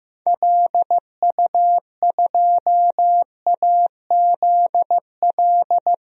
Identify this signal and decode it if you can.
Morse code, heard as a keyed tone, about 15 wpm, 700 Hz. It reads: LU2AZL
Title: DXCC-100 Call Signs - 6 Characters - 15wpm